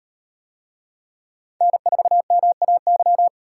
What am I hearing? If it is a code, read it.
Morse code, heard as a keyed tone, about 38 wpm, 700 Hz. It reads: N4MAY